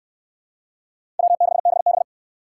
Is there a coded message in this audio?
Yes: Morse code from 1.2 to 2.0 s, heard as a keyed tone, about 69 wpm, 700 Hz. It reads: W6BB